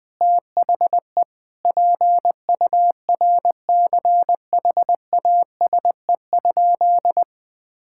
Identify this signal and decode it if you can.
Morse code, heard as a keyed tone, about 20 wpm, 700 Hz. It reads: THE PURCHASE?
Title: Sentences from Top 1000 Words - Review Part 5 - 20wpm